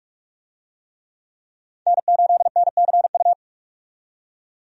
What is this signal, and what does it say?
Morse code, heard as a keyed tone, about 46 wpm, 700 Hz. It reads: N8NCU
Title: USA Call Signs - 5 Characters - 45wpm